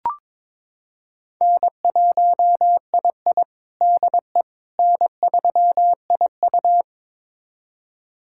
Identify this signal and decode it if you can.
Morse code, heard as a keyed tone, about 22 wpm, 700 Hz. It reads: N1II DE N3IU
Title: USA Callsign DE Callsign - 4 Characters - 22wpm